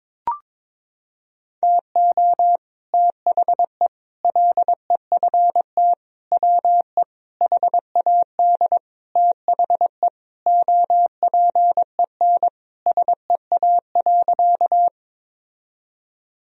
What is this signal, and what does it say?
Morse code, heard as a keyed tone, about 22 wpm, 700 Hz. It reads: TO THE LEFT WE HAD THE OPEN SEA.
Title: Sentences from Top 1000 Words - Review Part 1 - 22wpm